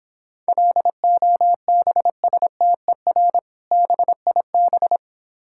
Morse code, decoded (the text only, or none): LOBSTER BIB